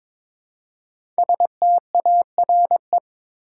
Morse code, heard as a keyed tone, about 22 wpm, 700 Hz. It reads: STARE
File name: 5 Letter Words - First 3 Letters Given - 22wpm